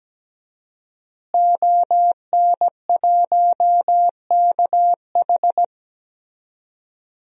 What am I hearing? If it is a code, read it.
Morse code, heard as a keyed tone, about 17 wpm, 700 Hz. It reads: ON1KH
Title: DXCC-100 Call Signs - 5 Characters - Repeated 3x - 17wpm